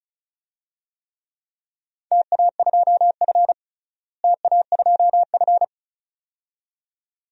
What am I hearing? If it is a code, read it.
Morse code, heard as a keyed tone, about 36 wpm, 700 Hz. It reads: TA2F TA2F
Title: DXCC-100 Call Signs - 4 Characters - Repeated 2x - 35wpm